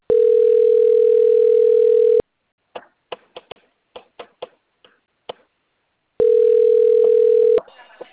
Ringback tone (North America), 440 + 480 Hz.